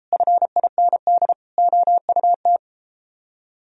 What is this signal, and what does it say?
Morse code, heard as a keyed tone, about 31 wpm, 700 Hz. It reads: FIND OUT